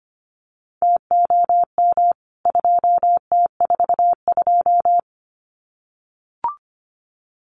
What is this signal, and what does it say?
Morse code, heard as a keyed tone, about 25 wpm, 700 Hz. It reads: TOM 2T42
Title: CWT Exchange 25wpm